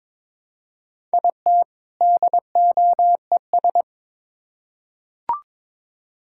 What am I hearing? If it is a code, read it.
Morse code, heard as a keyed tone, about 22 wpm, 700 Hz. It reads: IT DOES
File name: Build A Phrase - 2-Word Phrases - 22wpm